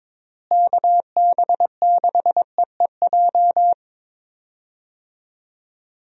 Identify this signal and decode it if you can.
Morse code, heard as a keyed tone, about 22 wpm, 700 Hz. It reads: KB6EEJ
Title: USA Call Signs - 6 Characters - Repeated 2x - 22wpm